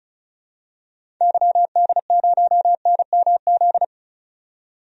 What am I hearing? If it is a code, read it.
Morse code, heard as a keyed tone, about 34 wpm, 700 Hz. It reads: YD0NMZ
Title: DXCC-100 Call Signs - 6 Characters - 35wpm